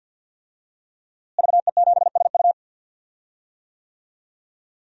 Morse code, heard as a keyed tone, about 53 wpm, 700 Hz. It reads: VE7SU
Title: DXCC-100 Call Signs - 5 Characters - Repeated 2x - 50wpm